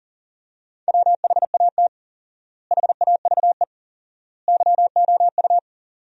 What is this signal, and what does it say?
Morse code, heard as a keyed tone, about 40 wpm, 700 Hz. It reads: WHAT HAVE YOU
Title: 4x Word-Spacing - Sentences from Top 200 Words - 40wpm